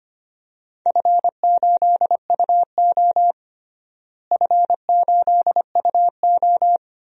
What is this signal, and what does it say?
Morse code, heard as a keyed tone, about 25 wpm, 700 Hz. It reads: F8UO F8UO
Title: DXCC-100 Call Signs - 4 Characters - Repeated 2x - 25wpm